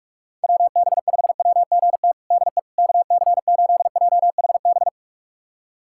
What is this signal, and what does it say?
Morse code, heard as a keyed tone, about 43 wpm, 700 Hz. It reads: WB5WGT DE KC8JHB
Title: USA Callsign DE Callsign 45wpm